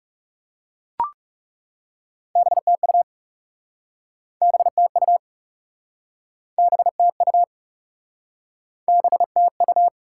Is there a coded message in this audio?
Yes: Morse code from 2.4 to 9.9 s, heard as a keyed tone, about 40 wpm, 700 Hz. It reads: BTU BTU BTU BTU